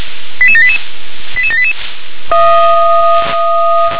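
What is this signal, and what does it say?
Five-tone selective call (ZVEI), heard as quick runs of five tones, tones of about 70 ms. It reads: 88788 88788
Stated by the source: Five-tone ZVEI selective call received by SDR.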